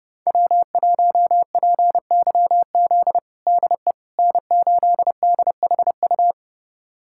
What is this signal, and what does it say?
Morse code, heard as a keyed tone, about 32 wpm, 700 Hz. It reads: W1PYZ DE N8DHU
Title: USA Callsign DE Callsign - 5 Characters - 30wpm